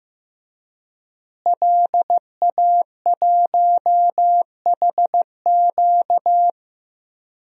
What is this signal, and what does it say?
Morse code, heard as a keyed tone, about 15 wpm, 700 Hz. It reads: LA1HQ
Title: DXCC-100 Call Signs - 5 Characters - Repeated 3x - 15wpm